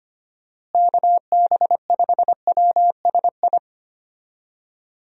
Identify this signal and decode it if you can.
Morse code, heard as a keyed tone, about 25 wpm, 700 Hz. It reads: KB5WSI
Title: USA Call Signs - 6 Characters - Repeated 3x - 25wpm